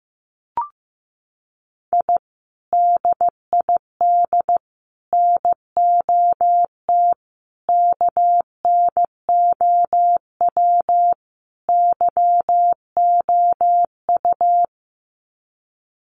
Morse code, heard as a keyed tone, about 15 wpm, 700 Hz. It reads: I DID NOT KNOW YOU
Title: Sets of 5 Words Encore2 15wpm